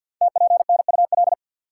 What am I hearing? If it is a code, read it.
Morse code, heard as a keyed tone, about 47 wpm, 700 Hz. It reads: TPNUL
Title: Code Groups - 5 Letters - 50wpm